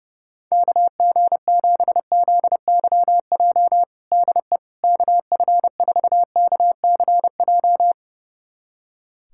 Morse code, heard as a keyed tone, about 30 wpm, 700 Hz. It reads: KG7ZYJ DE KF4KCJ
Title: USA Callsign DE Callsign - 6 Characters - 30wpm